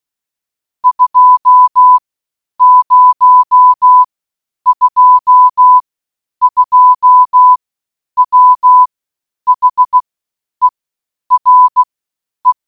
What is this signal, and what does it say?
Morse code, heard as a keyed tone, about 16 wpm, spaced as for 9 wpm, 1000 Hz. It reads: 2022 WHERE